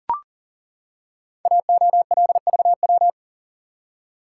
Morse code, heard as a keyed tone, about 38 wpm, 700 Hz. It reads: AOLVW
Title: Code Groups - 5 Letters - 40wpm